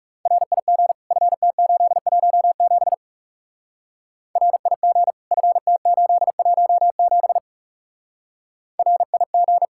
Morse code, heard as a keyed tone, about 36 wpm, 700 Hz. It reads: RIG FT817 RIG FT817 RIG
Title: Speed-Racing Common QSO Phrases Encore - 30wpm